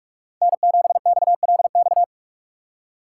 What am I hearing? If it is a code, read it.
Morse code, heard as a keyed tone, about 42 wpm, 700 Hz. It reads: N7XLX
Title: USA Call Signs - 5 Characters - 2x Character Spacing - 45wpm